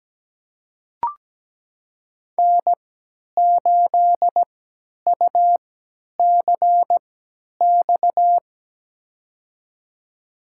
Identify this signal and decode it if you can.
Morse code, heard as a keyed tone, about 17 wpm, spaced as for 10 wpm, 700 Hz. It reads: N8UCX
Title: USA Call Signs - 5 Characters - 3x Character Spacing - 17wpm